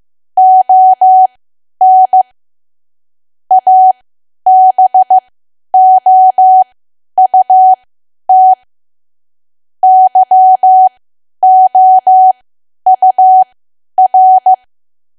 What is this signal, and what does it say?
Morse code, heard as a keyed tone, about 15 wpm, spaced as for 10 wpm, 750 Hz. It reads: ON ABOUT YOUR
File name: cwa / W1AW-Level2 / 10WPM